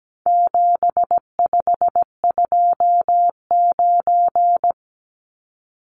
Morse code, heard as a keyed tone, about 17 wpm, 700 Hz. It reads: 7529
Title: Numbers - 4-Digit 17wpm